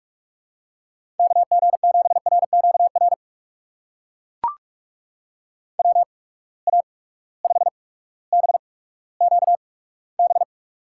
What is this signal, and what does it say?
Morse code, heard as a keyed tone, about 44 wpm, spaced as for 19 wpm, 700 Hz. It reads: KG7RQR WA5BQB
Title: USA Call Signs - 6 Characters - 8x Character Spacing - 45wpm